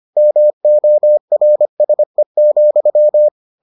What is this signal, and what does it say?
Morse code, heard as a keyed tone, about 25 wpm, 600 Hz. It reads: MORSE,